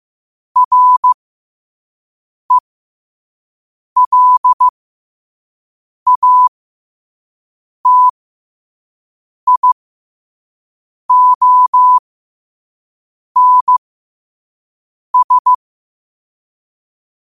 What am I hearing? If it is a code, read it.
Morse code, heard as a keyed tone, about 15 wpm, spaced as for 5 wpm, 1000 Hz. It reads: RELATIONS
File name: qotdfiles / 5wpm